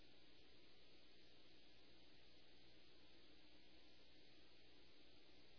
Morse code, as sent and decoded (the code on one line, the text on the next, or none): none